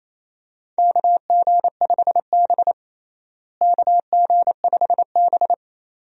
Morse code, heard as a keyed tone, about 28 wpm, 700 Hz. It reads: KG5B KG5B
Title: USA Call Signs - 4 Characters - Repeated 2x - 28wpm